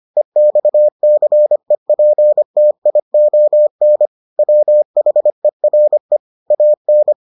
Morse code, heard as a keyed tone, about 25 wpm, 600 Hz. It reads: EXCEPTION WHERE AN